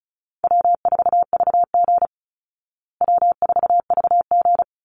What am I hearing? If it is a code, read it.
Morse code, heard as a keyed tone, about 34 wpm, 700 Hz. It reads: W4VG W4VG
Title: Easy Call Signs Repeated 2x - 35wpm